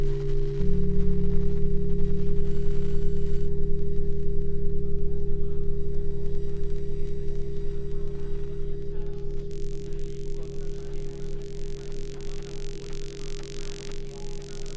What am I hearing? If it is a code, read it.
Dial tone (Japan), 400 Hz.